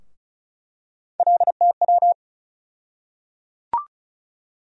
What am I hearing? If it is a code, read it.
Morse code, heard as a keyed tone, about 35 wpm, 700 Hz. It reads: LTW
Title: USA Call Sign Postfixes - 3 Letters 35wpm